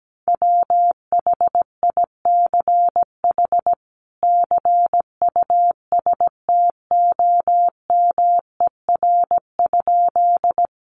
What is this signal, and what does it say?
Morse code, heard as a keyed tone, about 17 wpm, 700 Hz. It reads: WHICH CUSTOMER?
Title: Sentences from Top 1100 Words 17wpm